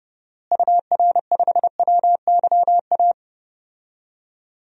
Morse code, heard as a keyed tone, about 32 wpm, 700 Hz. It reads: UR5WYA
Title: DXCC-100 Call Signs - 6 Characters - Repeated 2x - 30wpm